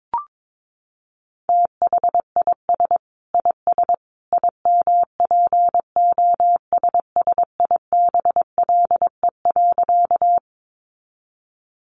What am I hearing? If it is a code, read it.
Morse code, heard as a keyed tone, about 22 wpm, 700 Hz. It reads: THIS IS IMPOSSIBLE.
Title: A Sound Of Thunder 22wpm